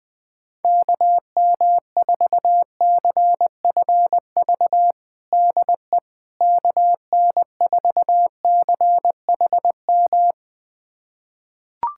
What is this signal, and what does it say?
Morse code, heard as a keyed tone, about 21 wpm, 700 Hz. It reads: KM4CFV DE KN4CHM
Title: USA Callsign DE Callsign - 6 Characters - 20wpm